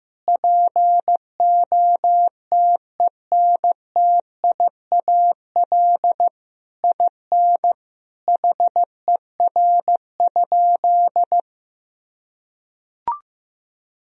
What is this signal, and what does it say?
Morse code, heard as a keyed tone, about 15 wpm, 700 Hz. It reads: POTENTIAL IN HER?
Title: Sentences from Top 1000 Words - Review Part 4 - 15wpm